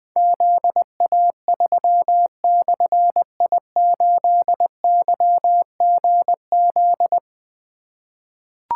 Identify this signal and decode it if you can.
Morse code, heard as a keyed tone, about 20 wpm, 700 Hz. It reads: ZA3/I8YGZ
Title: Difficult Call Signs 20wpm